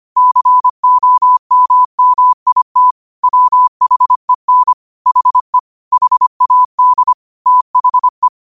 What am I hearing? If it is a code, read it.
Morse code, heard as a keyed tone, about 25 wpm, 1000 Hz. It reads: COMMIT WHEN HE HAD THE
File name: qotdfiles / 25wpm